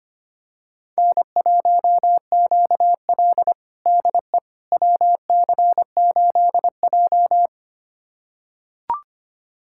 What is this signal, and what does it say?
Morse code, heard as a keyed tone, about 25 wpm, 700 Hz. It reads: N1QL DE WC8J
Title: USA Callsign DE Callsign - 4 Characters - 25wpm